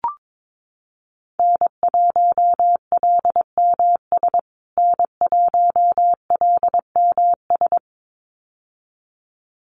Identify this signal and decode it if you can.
Morse code, heard as a keyed tone, about 22 wpm, 700 Hz. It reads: N1LMS N1LMS
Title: USA Call Signs Repeated 2x 22wpm